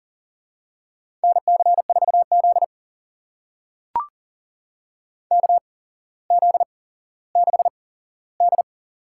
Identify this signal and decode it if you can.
Morse code, heard as a keyed tone, about 40 wpm, spaced as for 17 wpm, 700 Hz. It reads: NC4Z KZ6D
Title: USA Call Signs - 4 Characters - 8x Character Spacing - 40wpm